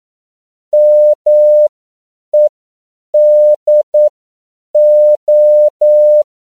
Morse code, heard as a keyed tone, about 9 wpm, spaced as for 7 wpm, 600 Hz. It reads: MEDO